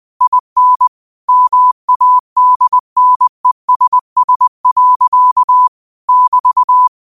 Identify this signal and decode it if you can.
Morse code, heard as a keyed tone, about 20 wpm, 1000 Hz. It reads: IN MADNESS. <BT>
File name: qotdfiles / 20wpm